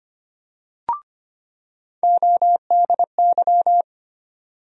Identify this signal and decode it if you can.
Morse code, heard as a keyed tone, about 25 wpm, 700 Hz. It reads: ODY
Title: USA Call Sign Postfixes - 3 Letters 25wpm